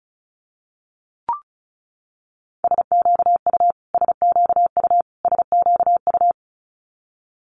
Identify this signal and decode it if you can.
Morse code, heard as a keyed tone, about 36 wpm, 700 Hz. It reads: SQU SQU SQU
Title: Multi-letter Phonemes Repeated 3x 35wpm